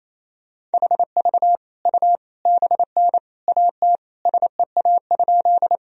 Morse code, heard as a keyed tone, about 28 wpm, 700 Hz. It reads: HV U BN AT SEA?
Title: Sentences from Top 300 Words and Abbreviations 28wpm